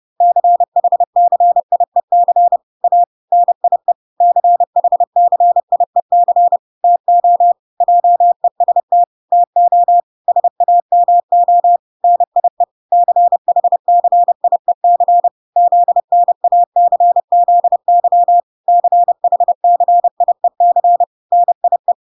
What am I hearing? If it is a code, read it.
Morse code, heard as a keyed tone, about 30 wpm, 700 Hz. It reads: CHCIEC A NIE CHCIEC TO JEST TO SAMO NIE CHCIEC ZNACZY CHCIEC NIE